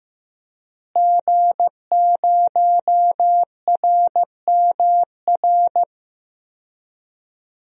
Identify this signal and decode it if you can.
Morse code, heard as a keyed tone, about 16 wpm, 700 Hz. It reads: G0RMR
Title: DXCC-100 Call Signs - 5 Characters - Repeated 3x - 15wpm